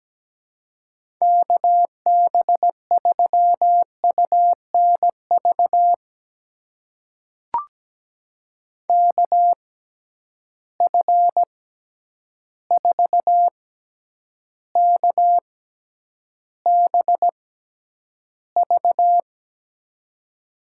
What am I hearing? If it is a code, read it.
Morse code, heard as a keyed tone, about 17 wpm, spaced as for 9 wpm, 700 Hz. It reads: KB3UNV KF4KBV
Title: USA Call Signs - 6 Characters - 6x Character Spacing - 17wpm